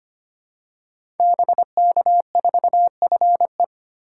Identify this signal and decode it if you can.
Morse code, heard as a keyed tone, about 25 wpm, 700 Hz. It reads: BK4FE
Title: Code Groups - 5 Letter:Numbers - 25wpm